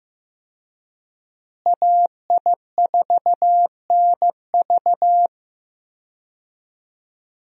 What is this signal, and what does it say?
Morse code, heard as a keyed tone, about 15 wpm, 700 Hz. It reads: AI4NV
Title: USA Call Signs - 5 Characters - Repeated 2x - 15wpm